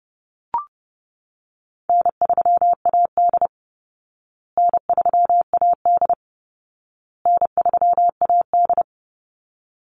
Morse code, heard as a keyed tone, about 30 wpm, 700 Hz. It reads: N3AD N3AD N3AD
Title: Easy Call Signs Repeated 3x - 30wpm